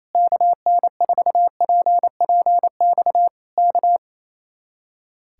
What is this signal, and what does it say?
Morse code, heard as a keyed tone, about 27 wpm, 700 Hz. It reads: KN4PPX K
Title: Speed-Racing Common QSO Phrases Encore - 28wpm